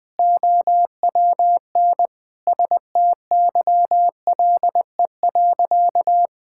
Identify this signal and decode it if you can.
Morse code, heard as a keyed tone, about 21 wpm, 700 Hz. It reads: OWN STYLE.
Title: Sentences from Top 1000 Words - Review Part 5 - 20wpm